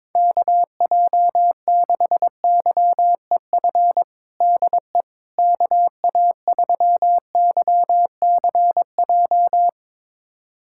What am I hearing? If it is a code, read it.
Morse code, heard as a keyed tone, about 22 wpm, 700 Hz. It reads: KJ6YEF DE KA3YCJ
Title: USA Callsign DE Callsign 22wpm